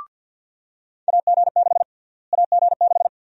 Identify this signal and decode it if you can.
Morse code, heard as a keyed tone, about 49 wpm, 700 Hz. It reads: AG6 AG6
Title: USA Call Sign Prefixes - Repeated 2x - 50wpm